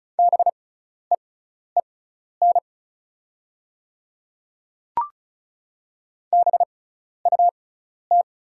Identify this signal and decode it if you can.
Morse code, heard as a keyed tone, about 36 wpm, spaced as for 12 wpm, 700 Hz. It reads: BEEN BUT
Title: 6x Character Spacing - Top 100 Words - 35wpm